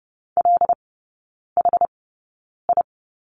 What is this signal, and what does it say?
Morse code, heard as a keyed tone, about 30 wpm, spaced as for 9 wpm, 700 Hz. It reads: LHI
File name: Lesson 17e - Letter u - Three Characters 30wpm